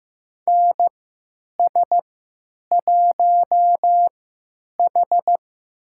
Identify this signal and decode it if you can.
Morse code, heard as a keyed tone, about 15 wpm, spaced as for 9 wpm, 700 Hz. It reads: NS1H
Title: USA Call Signs - 3x Character Spacing - 15wpm